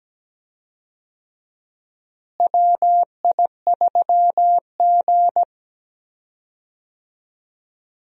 Morse code, heard as a keyed tone, about 17 wpm, 700 Hz. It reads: WI3G
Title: USA Call Signs - 4 Characters - Repeated 3x - 17wpm